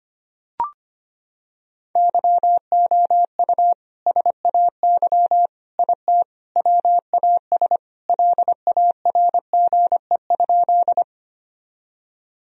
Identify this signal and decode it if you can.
Morse code, heard as a keyed tone, about 25 wpm, 700 Hz. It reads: YOU SAY IT WAS LARGE?
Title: Sentences from Top 1000 Words - Review Part 1 - 25wpm